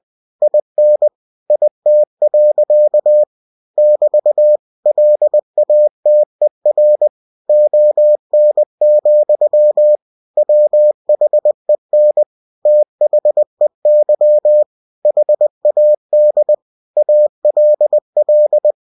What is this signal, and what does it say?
Morse code, heard as a keyed tone, about 20 wpm, 600 Hz. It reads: IN IT. <BT> LATER ON, WHEN THEY HAD ALL
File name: cw-books / winnie-the-pooh / winnie-20wpm-page-0109